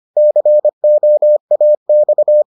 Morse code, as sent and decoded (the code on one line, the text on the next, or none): -.-. --- .- -..-
COAX